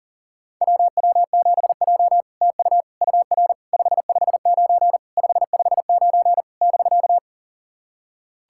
Morse code, heard as a keyed tone, about 44 wpm, 700 Hz. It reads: WW7J TU UR 559 559 <BK>
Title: SOTA QSO Practice 40wpm